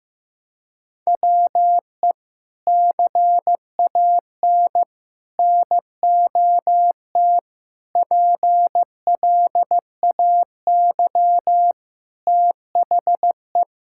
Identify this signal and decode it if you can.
Morse code, heard as a keyed tone, about 15 wpm, 700 Hz. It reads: WE CAN NOT PLAY THE
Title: Sentences from Top 1000 Words - Review Part 6 - 15wpm